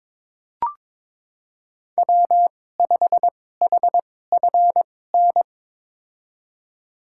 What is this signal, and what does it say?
Morse code, heard as a keyed tone, about 22 wpm, spaced as for 16 wpm, 700 Hz. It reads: W5HFN
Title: USA Call Signs - 2x Character Spacing - 22wpm